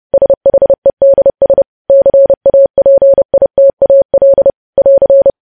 Morse code, heard as a keyed tone, about 30 wpm, 550 Hz. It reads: SHEDS CAPITAL <AR>
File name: podcast / News headlines-1x-30x30